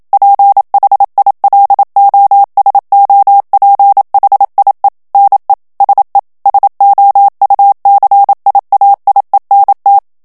Morse code, heard as a keyed tone, about 27 wpm, 800 Hz. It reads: PHILOSOPHIE NE SE SOUCIAIENT
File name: FRANCAIS TEXTE CLAIR 2 VITESSE 26 MOTS MINUTES